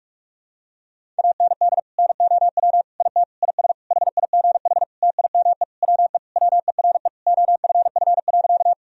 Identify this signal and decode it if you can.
Morse code, heard as a keyed tone, about 41 wpm, 700 Hz. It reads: AND NOW IT IS HIGH TIME WE WERE OFF.
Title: Sentences from Top 1000 Words - Review Part 1 - 45wpm